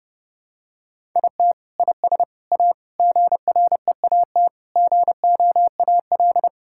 Morse code, heard as a keyed tone, about 30 wpm, 700 Hz. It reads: IT IS A GREAT GOAL
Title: Sentences from Top 1000 Words - Review Part 5 - 30wpm